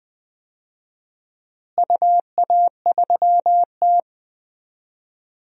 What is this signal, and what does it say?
Morse code, heard as a keyed tone, about 20 wpm, 700 Hz. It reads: UA3T
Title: DXCC-100 Call Signs - 4 Characters - Repeated 2x - 20wpm